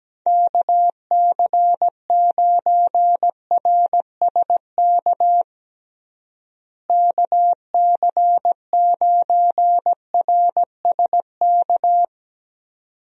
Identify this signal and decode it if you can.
Morse code, heard as a keyed tone, about 17 wpm, 700 Hz. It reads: KC9RSK KC9RSK
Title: USA Call Signs - 6 Characters - Repeated 2x - 17wpm